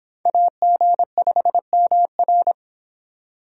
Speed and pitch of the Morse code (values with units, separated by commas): 26 wpm, 700 Hz